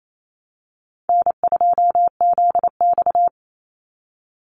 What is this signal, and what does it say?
Morse code, heard as a keyed tone, about 28 wpm, 700 Hz. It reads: N2ZX
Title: Easy Call Signs Repeated 3x - 28wpm